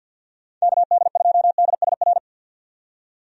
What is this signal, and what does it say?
Morse code, heard as a keyed tone, about 46 wpm, 700 Hz. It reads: KD2DSR
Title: USA Call Signs - 6 Characters - 50wpm